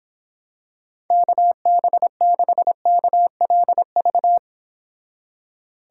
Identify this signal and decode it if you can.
Morse code, heard as a keyed tone, about 26 wpm, 700 Hz. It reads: KB6KLV